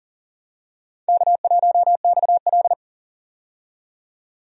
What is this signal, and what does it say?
Morse code, heard as a keyed tone, about 40 wpm, 700 Hz. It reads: K1XL
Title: USA Call Signs - 4 Characters - 40wpm